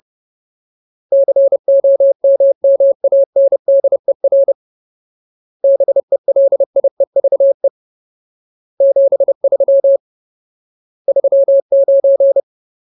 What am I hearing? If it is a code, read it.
Morse code, heard as a keyed tone, about 30 wpm, 550 Hz. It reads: COMMANDER BELIEVE 73 39
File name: Ultimate Word List / 30WPM Ultimate